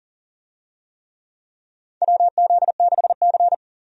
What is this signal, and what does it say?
Morse code, heard as a keyed tone, about 40 wpm, 700 Hz. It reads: WZ6C
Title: USA Call Signs - 4 Characters - 40wpm